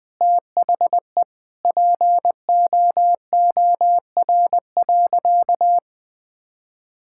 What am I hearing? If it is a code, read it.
Morse code, heard as a keyed tone, about 20 wpm, 700 Hz. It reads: THE POOR.
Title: Sentences from Top 1000 Words - Review Part 6 - 20wpm